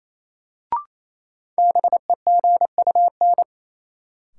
Morse code, heard as a keyed tone, about 28 wpm, 700 Hz. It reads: BEGUN
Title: Sentences from Top 1000 Words - Review Part 6 - New Word Drill - 28wpm